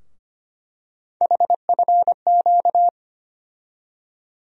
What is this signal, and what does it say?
Morse code, heard as a keyed tone, about 25 wpm, 700 Hz. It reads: HFQ
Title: USA Call Sign Postfixes - 3 Letters 25wpm